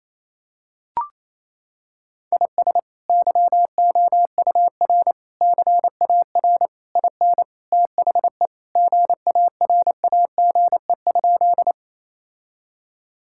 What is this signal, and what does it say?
Morse code, heard as a keyed tone, about 28 wpm, 700 Hz. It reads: IS YOUR CAR IN THE GARAGE?
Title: Sentences from Top 1600 Words 28wpm